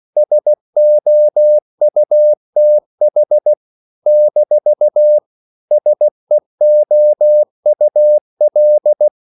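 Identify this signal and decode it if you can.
Morse code, heard as a keyed tone, about 16 wpm, 600 Hz. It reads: SOUTH - SEOUL
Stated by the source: CW practice of national capitals at 16 words per minute, from ON6ZQ.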